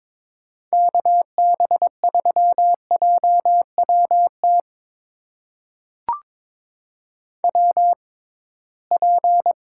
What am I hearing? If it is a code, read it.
Morse code, heard as a keyed tone, about 21 wpm, spaced as for 16 wpm, 700 Hz. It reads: KB3JWT WP
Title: USA Call Signs - 6 Characters - 6x Character Spacing - 22wpm